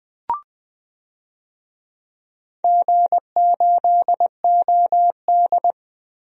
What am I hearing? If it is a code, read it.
Morse code, heard as a keyed tone, about 20 wpm, 700 Hz. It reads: G8OD
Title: DXCC-100 Call Signs - 4 Characters - Repeated 3x - 20wpm